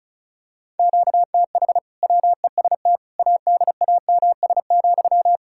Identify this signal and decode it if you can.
Morse code, heard as a keyed tone, about 34 wpm, 700 Hz. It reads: QTH WEST ADAMS,
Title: Practice QSO - Exchange 1 - Beginning - 35wpm